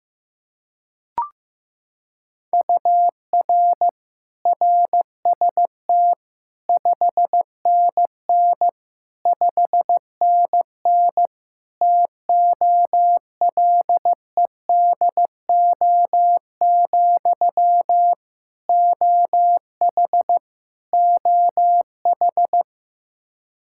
Morse code, heard as a keyed tone, about 15 wpm, 700 Hz. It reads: UR RST 5NN 5NN TOLEDO, OH OH
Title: Practice QSO - Exchange 1 - Beginning - 15wpm